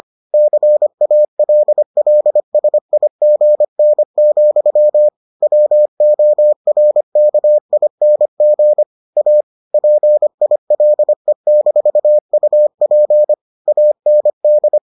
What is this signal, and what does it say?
Morse code, heard as a keyed tone, about 25 wpm, 600 Hz. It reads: CALLSIGN, WORKING A PILE-UP AND